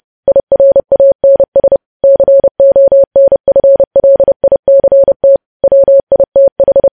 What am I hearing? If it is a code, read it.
Morse code, heard as a keyed tone, about 30 wpm, 550 Hz. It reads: IRANS CONFLICT WITH
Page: news headlines in morse code at 30 wpm